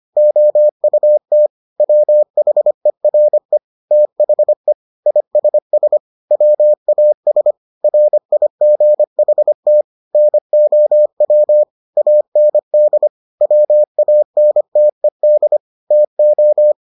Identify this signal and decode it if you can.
Morse code, heard as a keyed tone, about 25 wpm, 600 Hz. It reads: OUT WHERE THE ISS WAS RIGHT NOW AND WANTED TO